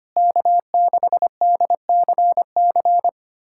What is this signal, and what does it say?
Morse code, heard as a keyed tone, about 25 wpm, 700 Hz. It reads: K6DCC
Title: USA Callsign DE Callsign - 5 Characters - 25wpm